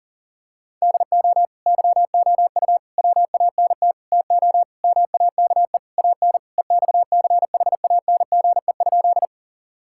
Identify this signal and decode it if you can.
Morse code, heard as a keyed tone, about 37 wpm, 700 Hz. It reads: DO YOU WANT TO MAKE AN EXCHANGE?
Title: Sentences from Top 1000 Words - Review Part 6 - 40wpm